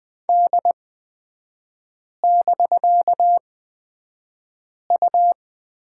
Morse code, heard as a keyed tone, about 20 wpm, spaced as for 5 wpm, 700 Hz. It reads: D<BK>U
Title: 250 Lesson 40e - BK - Three Characters 20wpm